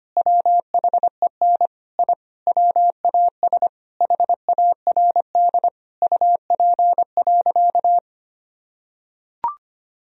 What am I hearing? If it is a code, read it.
Morse code, heard as a keyed tone, about 25 wpm, 700 Hz. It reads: WHEN I WAS HARD UP.
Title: Sentences from Top 1000 Words - Review Part 1 - 25wpm